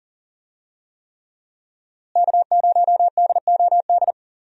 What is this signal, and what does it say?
Morse code, heard as a keyed tone, about 38 wpm, 700 Hz. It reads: K0DOD
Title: USA Call Signs - 5 Characters - Repeated 3x - 40wpm